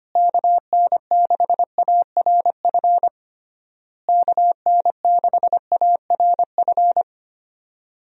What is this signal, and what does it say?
Morse code, heard as a keyed tone, about 25 wpm, 700 Hz. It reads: KN6ARF KN6ARF
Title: USA Call Signs - 6 Characters - Repeated 3x - 25wpm